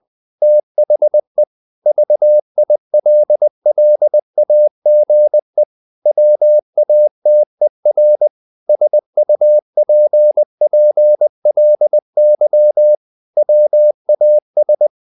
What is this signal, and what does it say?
Morse code, heard as a keyed tone, about 20 wpm, 600 Hz. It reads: THE VILLAGE WATER SUPPLY WAS
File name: the old man and the sea / the old man and the sea part 1 20wpm